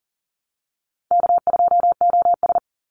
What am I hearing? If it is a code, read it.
Morse code, heard as a keyed tone, about 38 wpm, 700 Hz. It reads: K2OS
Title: Call Signs - 40wpm